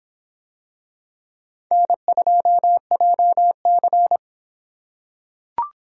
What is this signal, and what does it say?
Morse code, heard as a keyed tone, about 26 wpm, 700 Hz. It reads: N2JC